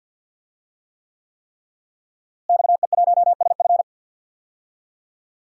Morse code, heard as a keyed tone, about 48 wpm, 700 Hz. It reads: XE1SF
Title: DXCC-100 Call Signs - 5 Characters - Repeated 3x - 50wpm